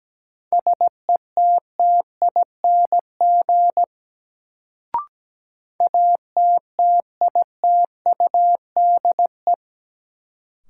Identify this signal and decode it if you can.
Morse code, heard as a keyed tone, about 17 wpm, 700 Hz. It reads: SETTING ATTITUDE
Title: Sentences from Top 1300 Words - New Word Drill - 17wpm